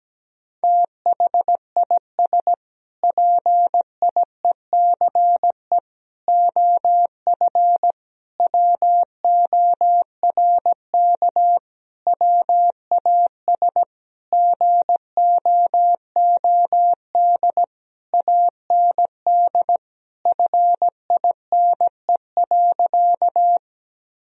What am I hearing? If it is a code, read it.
Morse code, heard as a keyed tone, about 17 wpm, 700 Hz. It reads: THIS PIECE OF WORK WAS GOOD AND FINE.
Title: Sentences from Top 1000 Words - Review Part 2 - 17wpm